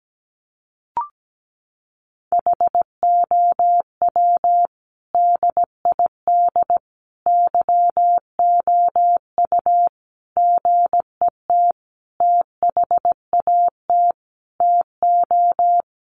Morse code, heard as a keyed tone, about 17 wpm, 700 Hz. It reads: HOW DID YOU GET THAT TO
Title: Sentences from Top 900 Words 17wpm